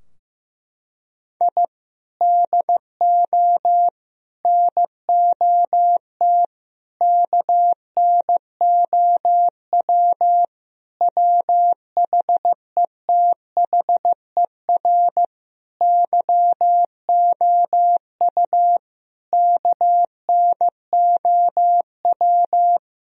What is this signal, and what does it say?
Morse code, heard as a keyed tone, about 15 wpm, 700 Hz. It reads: I DO NOT KNOW WHETHER YOU KNOW
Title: Sentences from Top 1000 Words - Review Part 3 - 15wpm